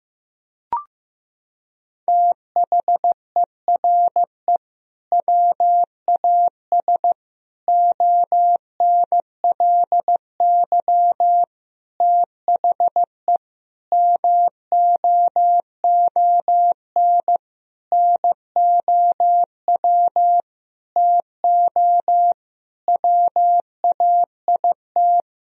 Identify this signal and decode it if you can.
Morse code, heard as a keyed tone, about 14 wpm, 700 Hz. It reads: THERE WAS ONLY THE MOON NOW TO WAIT
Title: Sentences from Top 1000 Words - Review Part 3 - 15wpm